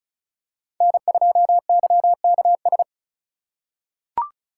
Morse code, heard as a keyed tone, about 36 wpm, 700 Hz. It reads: N2YKS
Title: USA Call Signs - 5 Characters - 6x Character Spacing - 35wpm